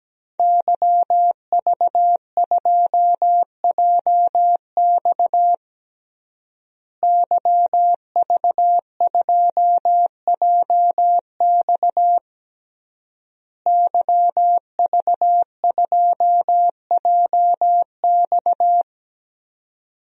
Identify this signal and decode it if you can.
Morse code, heard as a keyed tone, about 17 wpm, 700 Hz. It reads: YV2JX YV2JX YV2JX
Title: DXCC-100 Call Signs - 5 Characters - Repeated 3x - 17wpm